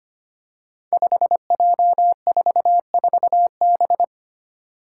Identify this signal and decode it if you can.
Morse code, heard as a keyed tone, about 25 wpm, 700 Hz. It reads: 5J44B
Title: Five Challenging Letter-Number Combinations 25wpm